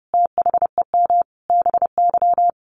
Morse code, heard as a keyed tone, about 30 wpm, 700 Hz. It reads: THEM BY
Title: Sentences from Top 800 Words 30wpm